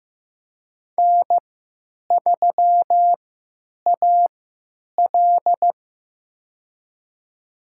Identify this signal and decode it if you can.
Morse code, heard as a keyed tone, about 15 wpm, spaced as for 9 wpm, 700 Hz. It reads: N3AL
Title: USA Call Signs - 4 Characters - 3x Character Spacing - 15wpm